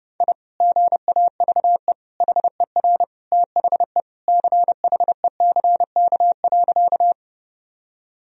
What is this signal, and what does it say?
Morse code, heard as a keyed tone, about 32 wpm, 700 Hz. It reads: I GAVE HER THE CHECK.
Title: Sentences from Top 1000 Words - Review Part 3 - 30wpm